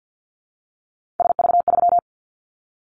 Morse code, heard as a keyed tone, about 50 wpm, 700 Hz. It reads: SV3
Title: Callsign Prefixes for Top 50 DXCC - 50wpm